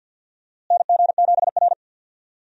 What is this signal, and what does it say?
Morse code, heard as a keyed tone, about 46 wpm, 700 Hz. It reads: NG7R
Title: USA Call Signs - 4 Characters - Repeated 2x - 50wpm